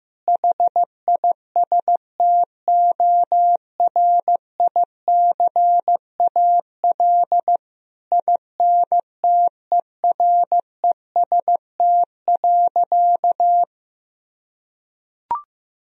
Morse code, heard as a keyed tone, about 15 wpm, 700 Hz. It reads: HISTORICAL INTEREST.
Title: Sentences from Top 1000 Words - Review Part 4 - 15wpm